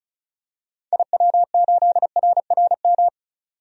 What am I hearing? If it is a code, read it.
Morse code, heard as a keyed tone, about 36 wpm, 700 Hz. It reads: IW8RRM
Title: DXCC-100 Call Signs - 6 Characters - 35wpm